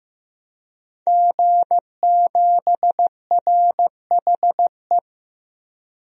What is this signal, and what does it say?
Morse code, heard as a keyed tone, about 16 wpm, 700 Hz. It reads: G7RHE
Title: DXCC-100 Call Signs - 5 Characters - Repeated 3x - 15wpm